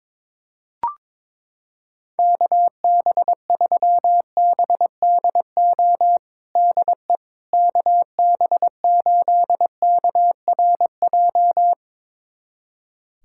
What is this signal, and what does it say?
Morse code, heard as a keyed tone, about 22 wpm, 700 Hz. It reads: KB3BDO DE KB8KRJ
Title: USA Callsign DE Callsign 22wpm